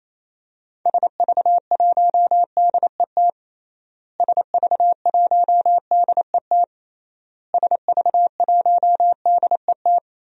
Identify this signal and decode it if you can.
Morse code, heard as a keyed tone, about 28 wpm, 700 Hz. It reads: SV1DET SV1DET SV1DET
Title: DXCC-100 Call Signs - 6 Characters - Repeated 3x - 28wpm